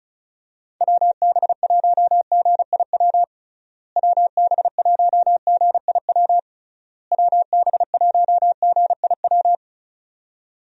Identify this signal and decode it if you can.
Morse code, heard as a keyed tone, about 36 wpm, 700 Hz. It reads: WB1GIW WB1GIW WB1GIW
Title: USA Call Signs - 6 Characters - Repeated 3x - 35wpm